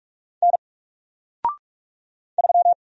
Morse code, heard as a keyed tone, about 46 wpm, 700 Hz. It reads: N 3
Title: Single Letter-Number 45wpm